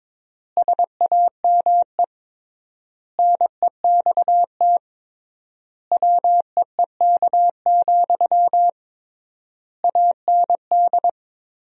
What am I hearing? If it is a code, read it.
Morse code, heard as a keyed tone, about 22 wpm, 700 Hz. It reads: SAME NEXT WEEK, AND
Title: 3x Word-Spacing - Sentences from Top 500 Words - 22wpm